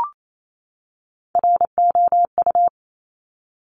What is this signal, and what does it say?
Morse code, heard as a keyed tone, about 28 wpm, 700 Hz. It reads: ROU
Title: Top Three Letter Combinations in Words 28wpm